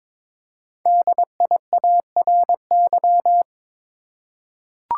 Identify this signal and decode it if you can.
Morse code, heard as a keyed tone, about 22 wpm, 700 Hz. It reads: DIARY
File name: 5 Letter Words - First 3 Letters Given - 22wpm